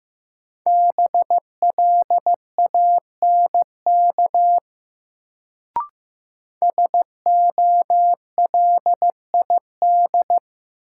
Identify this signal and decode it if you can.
Morse code, heard as a keyed tone, about 15 wpm, 700 Hz. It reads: BLANK SOLID
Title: Sentences from Top 1500 Words - New Word Drill Alt - 15wpm